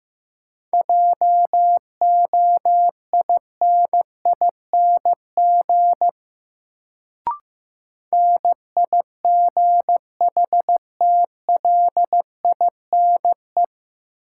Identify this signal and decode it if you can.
Morse code, heard as a keyed tone, about 15 wpm, 700 Hz. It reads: JOINING NIGHTLINE
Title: Sets of 5 Words - New Word Drill - 15wpm